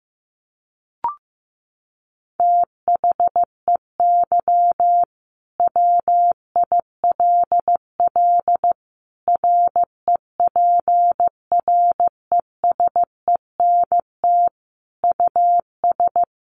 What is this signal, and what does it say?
Morse code, heard as a keyed tone, about 14 wpm, 700 Hz. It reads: THEY WILL REPRESENT US
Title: Sentences from Top 1200 Words 15wpm